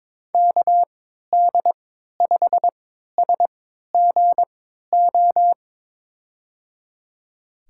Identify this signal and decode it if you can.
Morse code, heard as a keyed tone, about 22 wpm, spaced as for 13 wpm, 700 Hz. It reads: KD5SGO
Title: USA Call Signs - 6 Characters - 3x Character Spacing - 22wpm